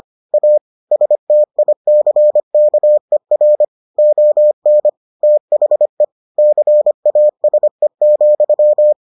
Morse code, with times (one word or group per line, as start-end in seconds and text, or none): 0.3-0.6 A
0.9-3.6 STICKER
4.0-4.9 ON
5.2-6.0 THE
6.4-8.9 CASE,